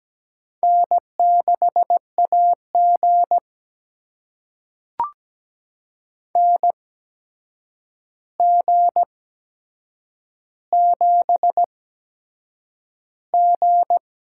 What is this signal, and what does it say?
Morse code, heard as a keyed tone, about 17 wpm, spaced as for 7 wpm, 700 Hz. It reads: N6AG NG7G
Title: USA Call Signs - 4 Characters - 8x Character Spacing - 17wpm